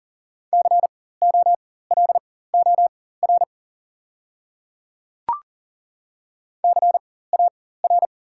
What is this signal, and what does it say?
Morse code, heard as a keyed tone, about 40 wpm, spaced as for 19 wpm, 690 Hz. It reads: COLOR CAR
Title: 4x Character Spacing - Top 500 Words - 40wpm